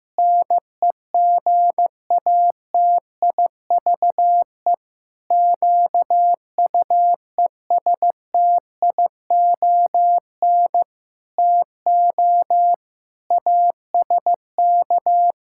Morse code, heard as a keyed tone, about 15 wpm, 700 Hz. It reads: NEGATIVE QUESTION TO ASK
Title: Sentences from Top 1000 Words - Review Part 6 - 15wpm